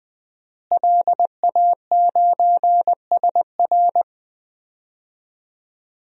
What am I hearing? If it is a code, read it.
Morse code, heard as a keyed tone, about 20 wpm, 700 Hz. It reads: LA9SR
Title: DXCC-100 Call Signs - 5 Characters - Repeated 3x - 20wpm